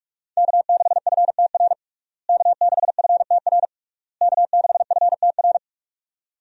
Morse code, heard as a keyed tone, about 46 wpm, 700 Hz. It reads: K6FTR K6FTR K6FTR
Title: USA Call Signs - 5 Characters - Repeated 3x - 45wpm